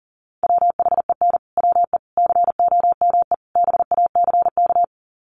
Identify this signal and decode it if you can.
Morse code, heard as a keyed tone, about 40 wpm, 700 Hz. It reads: WHEN WE COME BACK
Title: Sets of 4 Words 40wpm